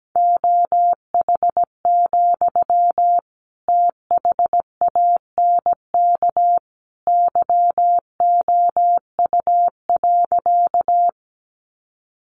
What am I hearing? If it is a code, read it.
Morse code, heard as a keyed tone, about 17 wpm, 700 Hz. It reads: OH, THANK YOU.
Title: Sentences from Top 600 Words 17wpm